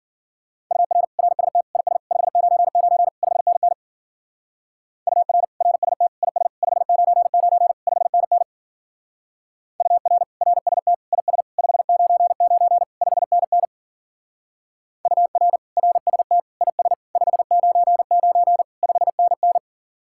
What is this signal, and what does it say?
Morse code, heard as a keyed tone, about 47 wpm, 700 Hz. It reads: UR RST IS 599 5NN UR RST IS 599 5NN UR RST IS 599 5NN UR RST IS 599 5NN
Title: Speed-Racing Common QSO Phrases Encore - 40wpm